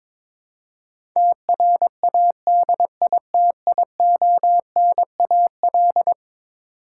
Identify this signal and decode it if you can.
Morse code, heard as a keyed tone, about 22 wpm, 700 Hz. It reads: TRADITIONAL
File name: practice / 22wpm / Eleven Letter Words - 22wpm